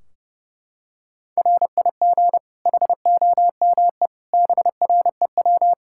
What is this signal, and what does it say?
Morse code, heard as a keyed tone, about 30 wpm, 700 Hz. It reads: RIG HOME BREW
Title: K1USN SST QSO Practice 30wpm